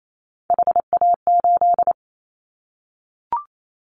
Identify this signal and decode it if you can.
Morse code, heard as a keyed tone, about 28 wpm, 700 Hz. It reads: HA8
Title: Callsign Prefixes for Top 25 DXCC - Repeated 3x - 28wpm